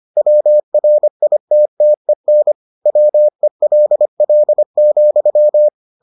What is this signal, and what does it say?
Morse code, heard as a keyed tone, about 25 wpm, 600 Hz. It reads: WRITTEN WELL,